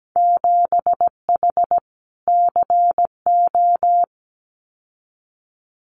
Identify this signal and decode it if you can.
Morse code, heard as a keyed tone, about 17 wpm, 700 Hz. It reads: 7H CO
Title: Winter Field Day Exchanges Encore 17wpm